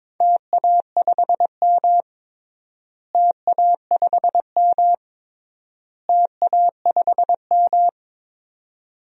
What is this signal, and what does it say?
Morse code, heard as a keyed tone, about 22 wpm, 700 Hz. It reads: TA5M TA5M TA5M
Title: DXCC-100 Call Signs - 4 Characters - Repeated 3x - 22wpm